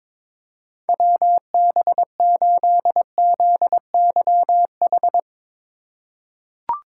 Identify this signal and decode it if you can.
Morse code, heard as a keyed tone, about 22 wpm, 700 Hz. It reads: WB8ZYH
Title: USA Call Signs - 6 Characters - 4x Character Spacing - 22wpm